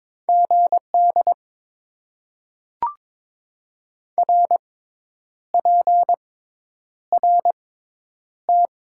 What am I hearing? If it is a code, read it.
Morse code, heard as a keyed tone, about 22 wpm, spaced as for 9 wpm, 700 Hz. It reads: GD RPRT
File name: Top 90 QSO Elements - 6x Character Spacing - 22wpm